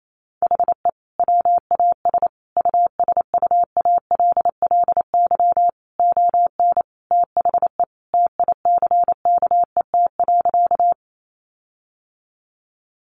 Morse code, heard as a keyed tone, about 28 wpm, 700 Hz. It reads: HE WAS USUALLY ON THE TICKET.
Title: Sentences from Top 1400 Words 28wpm